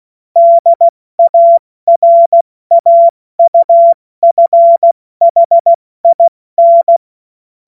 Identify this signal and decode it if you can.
Morse code, heard as a keyed tone, about 16 wpm, spaced as for 14 wpm, 680 Hz. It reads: DARAUFHIN
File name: Morsekurs / Witze / Witz 11. April 70 F80